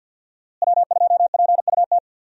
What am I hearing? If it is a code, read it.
Morse code, heard as a keyed tone, about 52 wpm, 700 Hz. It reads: W2PUT
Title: USA Call Signs - 5 Characters - Repeated 2x - 50wpm